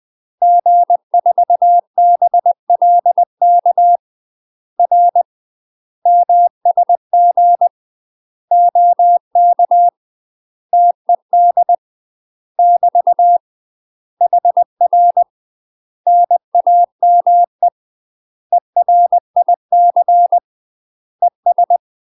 Morse code, heard as a keyed tone, about 20 wpm, 700 Hz. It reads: G4BLK R MSG OK TED <BT> HR NAME ERIC ES
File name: UK MASTER QSOs TRAINING / 20WPM / UK MASTER QSOs 0059